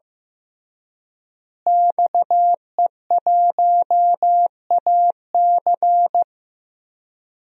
Morse code, heard as a keyed tone, about 16 wpm, 700 Hz. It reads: XE1AC
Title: DXCC-100 Call Signs - 5 Characters - Repeated 3x - 15wpm